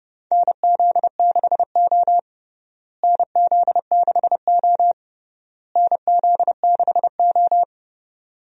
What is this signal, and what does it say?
Morse code, heard as a keyed tone, about 32 wpm, 700 Hz. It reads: NZ6O NZ6O NZ6O
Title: USA Call Signs - 4 Characters - Repeated 3x - 30wpm